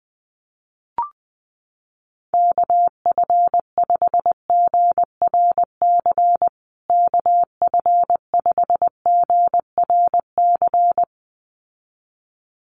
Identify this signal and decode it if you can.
Morse code, heard as a keyed tone, about 20 wpm, 700 Hz. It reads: KF5GRC KF5GRC
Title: USA Call Signs Repeated 2x 20wpm